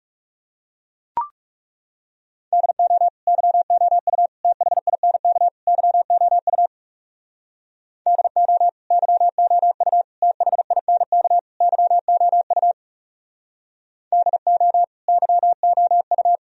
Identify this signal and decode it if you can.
Morse code, heard as a keyed tone, about 37 wpm, 700 Hz. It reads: DO YOU THINK YOU DO YOU THINK YOU DO YOU
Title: Speed-Racing - Sets of 4 Words 30wpm